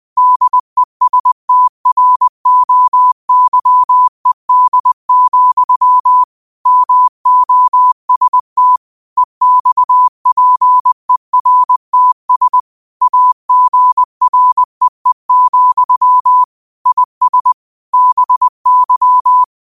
Morse code, heard as a keyed tone, about 20 wpm, 1000 Hz. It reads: DESTROYED, MOST EXPERTS AGREE, IS BY